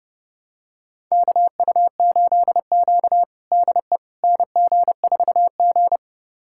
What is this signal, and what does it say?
Morse code, heard as a keyed tone, about 32 wpm, 700 Hz. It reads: KU8Q DE NG4G
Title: USA Callsign DE Callsign - 4 Characters - 30wpm